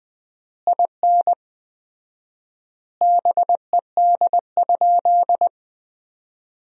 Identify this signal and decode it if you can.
Morse code, heard as a keyed tone, about 20 wpm, 700 Hz. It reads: IN BED?
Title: 4x Word-Spacing - Sentences from Top 500 Words - 20wpm